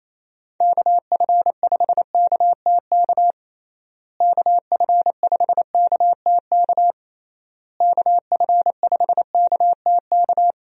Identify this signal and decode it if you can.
Morse code, heard as a keyed tone, about 27 wpm, 700 Hz. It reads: KF5KTK KF5KTK KF5KTK
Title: USA Call Signs - 6 Characters - Repeated 3x - 28wpm